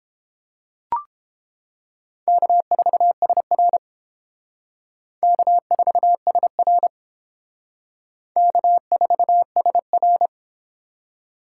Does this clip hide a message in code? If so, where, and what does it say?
Yes: Morse code from 2.3 to 10.3 s, heard as a keyed tone, about 30 wpm, 700 Hz. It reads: K4SR K4SR K4SR